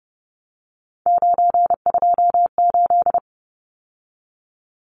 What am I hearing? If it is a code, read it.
Morse code, heard as a keyed tone, about 30 wpm, 700 Hz. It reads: 928
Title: Numbers - 3-Digit 30wpm